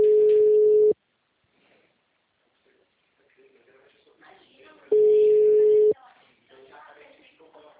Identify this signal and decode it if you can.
Ringback tone (Europe, 425 Hz), with speech.